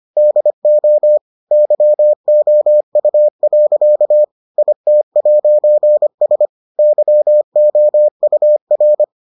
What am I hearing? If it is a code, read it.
Morse code, heard as a keyed tone, about 25 wpm, 600 Hz. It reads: DO YOU. IT'S YOUR